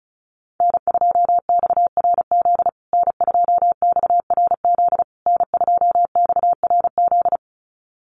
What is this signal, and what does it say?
Morse code, heard as a keyed tone, about 34 wpm, 700 Hz. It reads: N2XRZ N2XRZ N2XRZ
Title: USA Call Signs Repeated 3x 35wpm